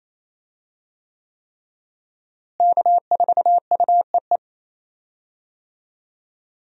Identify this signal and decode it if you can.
Morse code, heard as a keyed tone, about 28 wpm, 700 Hz. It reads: K4UEE
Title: USA Call Signs - 5 Characters - Repeated 3x - 28wpm